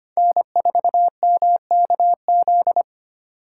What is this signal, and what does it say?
Morse code, heard as a keyed tone, about 29 wpm, 700 Hz. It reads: N4MKZ
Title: Speed Racing - USA 1x3 Call Signs 25wpm